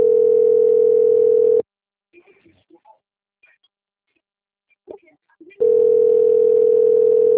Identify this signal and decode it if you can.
Ringback tone (North America), 440 + 480 Hz, with speech.
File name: out- - - - - .17851